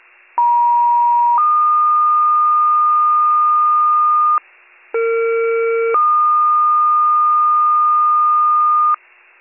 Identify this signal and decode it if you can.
Two-tone page, heard as a one-second tone, then a three-second tone, about 950 then 1220 Hz, about 460 then 1220 Hz, with noise. It is heard from 0.4 to 9.0 s.